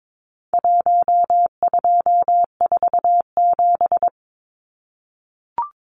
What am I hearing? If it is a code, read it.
Morse code, heard as a keyed tone, about 22 wpm, 700 Hz. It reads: 1247
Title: Numbers - 4-Digit 22wpm